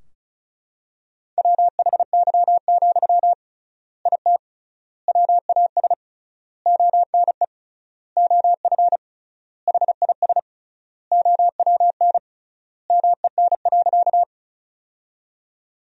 Morse code, heard as a keyed tone, about 36 wpm, 700 Hz. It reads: WHY, IT WAS ONE OF HIS OWN MEN.
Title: 3x Word-Spacing - Sentences from Top 300 Words - 35wpm